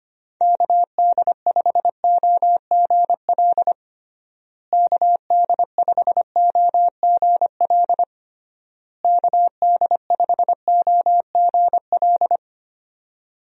Morse code, heard as a keyed tone, about 25 wpm, 700 Hz. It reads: KD5OGL KD5OGL KD5OGL
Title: USA Call Signs - 6 Characters - Repeated 3x - 25wpm